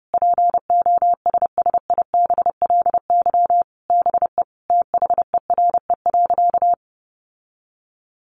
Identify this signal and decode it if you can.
Morse code, heard as a keyed tone, about 30 wpm, 700 Hz. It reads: POSSIBLY BE THERE.
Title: Sentences from Top 1300 Words 30wpm